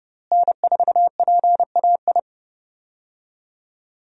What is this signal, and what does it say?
Morse code, heard as a keyed tone, about 31 wpm, 700 Hz. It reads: N4PAI
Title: USA Call Signs - 5 Characters - Repeated 2x - 30wpm